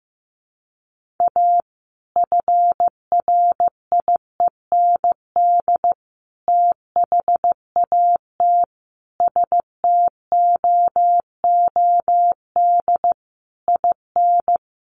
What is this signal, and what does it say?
Morse code, heard as a keyed tone, about 15 wpm, 700 Hz. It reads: A FRIEND THAT STOOD IN
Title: Sentences from Top 1500 Words 15wpm